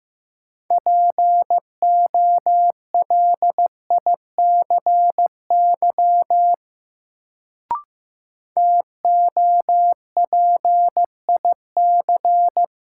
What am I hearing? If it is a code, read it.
Morse code, heard as a keyed tone, about 15 wpm, 700 Hz. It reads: POLICY TOPIC
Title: Sentences from Top 1200 Words - New Word Drill Alt - 15wpm